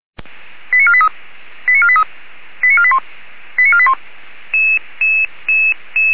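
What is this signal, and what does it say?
Five-tone selective call (ZVEI), heard as quick runs of five tones, tones of about 70 ms. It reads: 89483 89483 89481 89481
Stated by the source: ZVEI five-tone selective call 89235 received via SDR.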